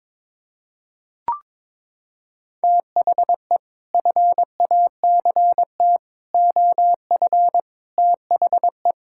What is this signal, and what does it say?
Morse code, heard as a keyed tone, about 22 wpm, 700 Hz. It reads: THE FACT OF THE
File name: practice / 22wpm / Sets of 8 Words - 22wpm